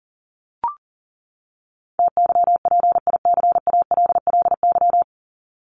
Morse code, heard as a keyed tone, about 40 wpm, 700 Hz. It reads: TYPICALLY
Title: LY Suffix - Top 100 Words with Base Words 40wpm